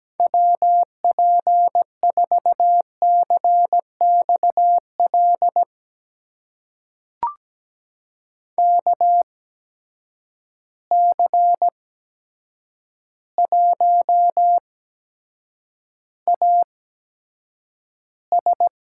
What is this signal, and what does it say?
Morse code, heard as a keyed tone, about 17 wpm, spaced as for 8 wpm, 700 Hz. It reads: WP4CXL KC1AS
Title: USA Call Signs - 6 Characters - 8x Character Spacing - 17wpm